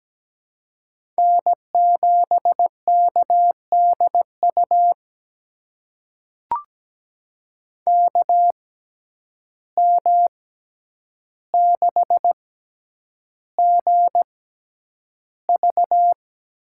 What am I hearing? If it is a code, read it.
Morse code, heard as a keyed tone, about 17 wpm, spaced as for 9 wpm, 700 Hz. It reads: N7KDU KM6GV
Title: USA Call Signs - 5 Characters - 6x Character Spacing - 17wpm